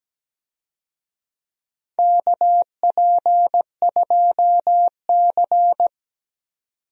Morse code, heard as a keyed tone, about 17 wpm, 700 Hz. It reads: KP2C
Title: DXCC-100 Call Signs - 4 Characters - 17wpm